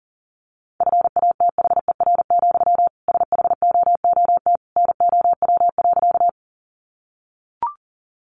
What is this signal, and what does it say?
Morse code, heard as a keyed tone, about 37 wpm, 700 Hz. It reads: FATHER, SHOOT NOW.
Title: Sentences from Top 1200 Words 40wpm